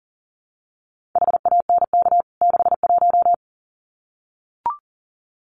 Morse code, heard as a keyed tone, about 40 wpm, 700 Hz. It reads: HANK 61
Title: CWT Exchange 40wpm